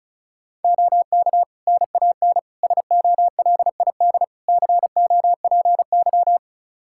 Morse code, heard as a keyed tone, about 34 wpm, 700 Hz. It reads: OK NAN SOLID COPY
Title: Practice QSO - Exchange 3 - More Info - 35wpm